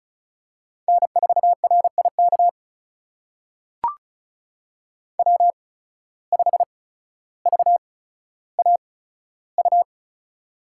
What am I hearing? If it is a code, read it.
Morse code, heard as a keyed tone, about 34 wpm, spaced as for 15 wpm, 700 Hz. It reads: N4RIK W5VAU
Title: USA Call Signs - 5 Characters - 8x Character Spacing - 35wpm